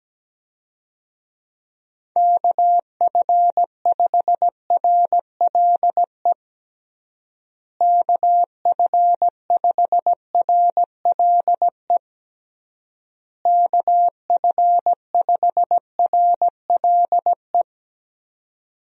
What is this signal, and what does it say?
Morse code, heard as a keyed tone, about 17 wpm, 700 Hz. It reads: KF5RLE KF5RLE KF5RLE
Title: USA Call Signs - 6 Characters - Repeated 3x - 17wpm